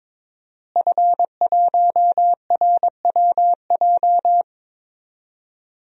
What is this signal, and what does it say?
Morse code, heard as a keyed tone, about 22 wpm, 700 Hz. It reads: F1RWJ
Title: DXCC-100 Call Signs - 5 Characters - Repeated 3x - 22wpm